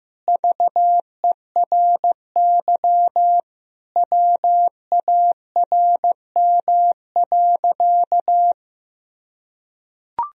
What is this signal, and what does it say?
Morse code, heard as a keyed tone, about 16 wpm, 700 Hz. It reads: VERY WARM.
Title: Sentences from Top 1000 Words - Review Part 2 - 15wpm